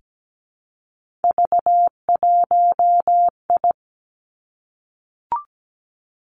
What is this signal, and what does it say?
Morse code, heard as a keyed tone, about 17 wpm, 700 Hz. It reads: V1I
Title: Three Challenging Letter-Number Combinations 17wpm